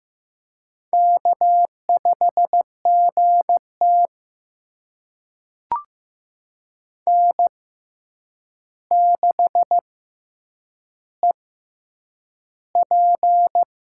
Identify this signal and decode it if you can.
Morse code, heard as a keyed tone, about 15 wpm, spaced as for 8 wpm, 700 Hz. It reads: K5GT N6EP
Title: USA Call Signs - 4 Characters - 6x Character Spacing - 15wpm